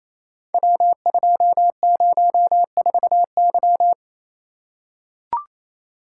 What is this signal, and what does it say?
Morse code, heard as a keyed tone, about 29 wpm, 700 Hz. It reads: W204Y
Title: Code Groups - 5 Letter:Numbers - 28wpm